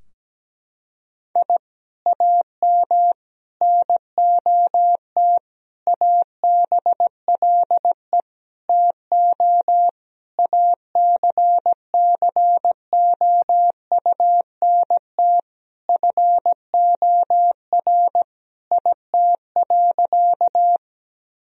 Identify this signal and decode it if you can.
Morse code, heard as a keyed tone, about 17 wpm, 700 Hz. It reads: I AM NOT ABLE TO ACCOUNT FOR IT.
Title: Sentences from Top 1000 Words - Review Part 2 - 17wpm